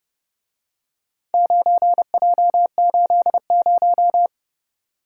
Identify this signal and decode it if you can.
Morse code, heard as a keyed tone, about 31 wpm, 700 Hz. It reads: 9J80